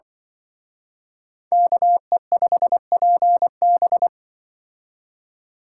Morse code, heard as a keyed tone, about 25 wpm, 700 Hz. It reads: KE5PB
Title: Speed Racing - USA 2x2 Call Signs 20wpm